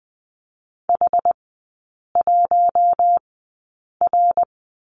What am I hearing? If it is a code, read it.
Morse code, heard as a keyed tone, about 20 wpm, spaced as for 8 wpm, 700 Hz. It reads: H1R
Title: 114 Lesson 19e - Letter w - Three Characters 20wpm